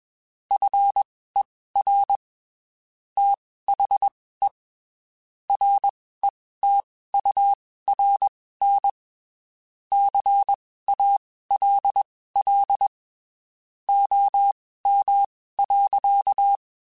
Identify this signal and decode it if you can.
Morse code, heard as a keyed tone, about 21 wpm, spaced as for 15 wpm, 800 Hz. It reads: FER THE RETURN CALL OM.